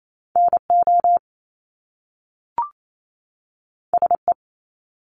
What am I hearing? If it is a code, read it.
Morse code, heard as a keyed tone, about 28 wpm, 700 Hz. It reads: NO SE
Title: Top Two Letter Combinations in Words 28wpm